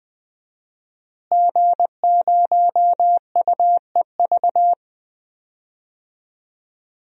Morse code, heard as a keyed tone, about 20 wpm, 700 Hz. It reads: G0UEV
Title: DXCC-100 Call Signs - 5 Characters - Repeated 3x - 20wpm